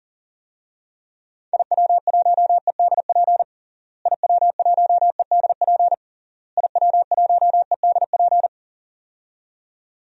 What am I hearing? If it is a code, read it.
Morse code, heard as a keyed tone, about 40 wpm, 700 Hz. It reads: IW1EDP IW1EDP IW1EDP
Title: DXCC-100 Call Signs - 6 Characters - Repeated 3x - 40wpm